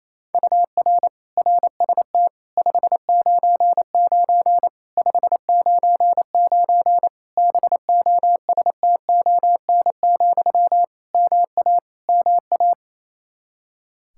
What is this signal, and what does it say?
Morse code, heard as a keyed tone, about 28 wpm, 700 Hz. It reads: UR RST 599 599 BOSTON, MA MA
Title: Practice QSO - Exchange 1 - Beginning - 28wpm